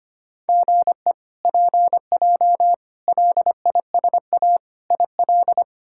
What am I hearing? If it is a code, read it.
Morse code, heard as a keyed tone, about 25 wpm, 700 Hz. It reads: GE PJ LISA IL
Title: K1USN SST QSO Practice - 25wpm